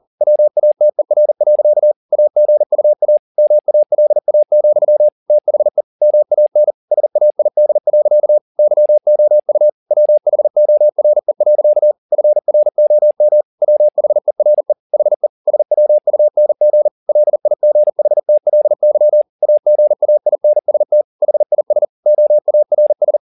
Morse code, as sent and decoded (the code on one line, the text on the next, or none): .-- .- - . .-. .-.-.- / .- --. ..- .- / -- .- .-.. .- --..-- / - .... / -- .- -. / ... .- .. -.. .-.-.- / -.-- --- ..- / .-- .... --- .-. .-.-.- / ..-. .-. --- -- / .-- .... .-. / .... / ... .-- ..- -. --. / .-.. .. --. .... - .-.. -.-- / .- --. .- .. -. ... - / .... .. ... / --- .- .-. ...
WATER. AGUA MALA, THE MAN SAID. YOU WHORE. FROM WHERE HE SWUNG LIGHTLY AGAINST HIS OARS